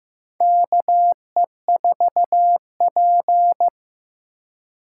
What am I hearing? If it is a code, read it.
Morse code, heard as a keyed tone, about 15 wpm, 700 Hz. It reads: KE4P
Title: USA Call Signs - 3x Character Spacing - 15wpm